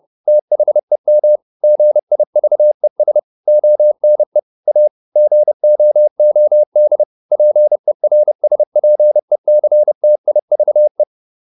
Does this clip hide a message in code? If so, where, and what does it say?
Yes: Morse code from 0.3 to 11.0 s, heard as a keyed tone, about 30 wpm, 600 Hz. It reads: THEM GIVES ONE A GOOD PERSPECTIVE